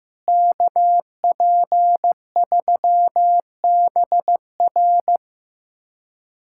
Morse code, heard as a keyed tone, about 16 wpm, 700 Hz. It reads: KP3BR
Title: USA Callsign DE Callsign - 5 Characters - 15wpm